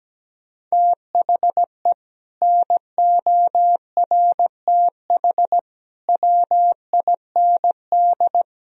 Morse code, heard as a keyed tone, about 17 wpm, 700 Hz. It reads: THE NORTH WIND
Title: Sentences from Top 1000 Words - Review Part 3 - 17wpm